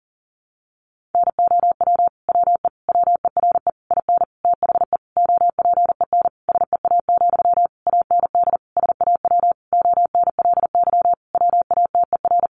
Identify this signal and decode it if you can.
Morse code, heard as a keyed tone, about 40 wpm, 700 Hz. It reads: NOW WE WERE IN THE OPEN SEA, AND SAW ONLY WATER
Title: Sentences from Top 1200 Words 40wpm